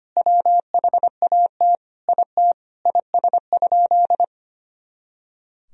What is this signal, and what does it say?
Morse code, heard as a keyed tone, about 25 wpm, 700 Hz. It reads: WHAT IT IS?
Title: Sentences from Top 100 Words and Abbreviations 25wpm